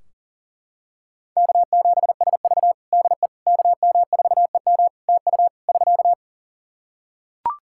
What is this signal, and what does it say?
Morse code, heard as a keyed tone, about 37 wpm, 700 Hz. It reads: K7SV DE KM4EM TU <SK>
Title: Practice QSO - Exchange 4 - Ending - 40wpm